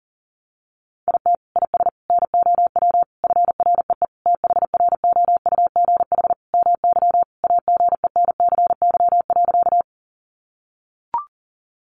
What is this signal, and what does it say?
Morse code, heard as a keyed tone, about 40 wpm, 700 Hz. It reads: IT IS NOW FREE THROUGH MY AGENCY.
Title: Sentences from Top 1300 Words 40wpm